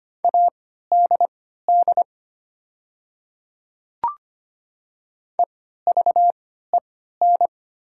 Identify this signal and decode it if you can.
Morse code, heard as a keyed tone, about 25 wpm, spaced as for 14 wpm, 690 Hz. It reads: ADD EVEN
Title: 3x Character Spacing - Top 300 Words - 25wpm